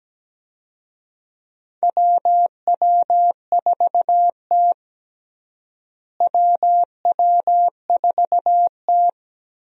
Morse code, heard as a keyed tone, about 17 wpm, 700 Hz. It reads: WW4T WW4T
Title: USA Call Signs - 4 Characters - Repeated 3x - 17wpm